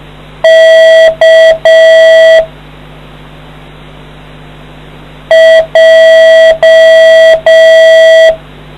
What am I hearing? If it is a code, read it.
Morse code, heard as a keyed tone, about 6 wpm, 650 Hz. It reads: K J